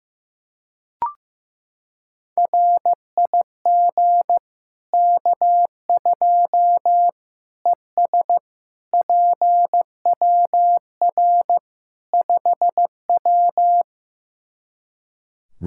Morse code, heard as a keyed tone, about 15 wpm, 700 Hz. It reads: RIG K2 ES PWR 5W
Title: Practice QSO - Exchange 2 - Info - 15wpm